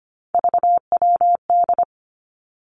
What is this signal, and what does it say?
Morse code, heard as a keyed tone, about 25 wpm, 700 Hz. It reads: VWD
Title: Three Challenging Letter-Number Combinations 25wpm